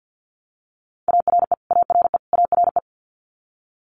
Morse code, heard as a keyed tone, about 50 wpm, 700 Hz. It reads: ARE ARE ARE
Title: Multi-letter Phonemes Repeated 3x 50wpm